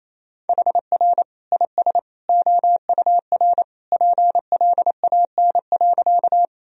Morse code, heard as a keyed tone, about 28 wpm, 700 Hz. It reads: HR IS OUR PLAN.
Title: Sentences from Top 500 Words and Abbreviations 28wpm